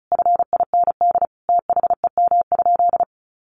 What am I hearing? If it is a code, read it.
Morse code, heard as a keyed tone, about 34 wpm, 700 Hz. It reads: FIND THEM?
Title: Sentences from Top 200 Words 35wpm